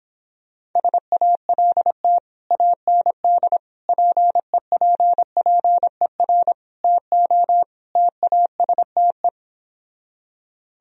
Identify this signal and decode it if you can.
Morse code, heard as a keyed tone, about 26 wpm, 700 Hz. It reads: SALT AND PEPPER TO TASTE